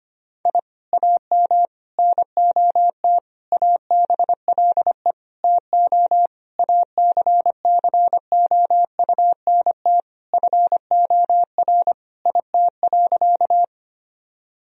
Morse code, heard as a keyed tone, about 25 wpm, 700 Hz. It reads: I AM NOT ABLE TO ACCOUNT FOR IT.
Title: Sentences from Top 1000 Words - Review Part 2 - 25wpm